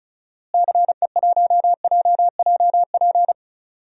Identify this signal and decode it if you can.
Morse code, heard as a keyed tone, about 36 wpm, 700 Hz. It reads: CE1JJP
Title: DXCC-100 Call Signs - 6 Characters - Repeated 2x - 35wpm